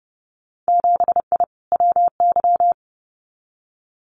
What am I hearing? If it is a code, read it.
Morse code, heard as a keyed tone, about 30 wpm, 700 Hz. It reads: 7I WY
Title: Winter Field Day Exchanges Encore 30wpm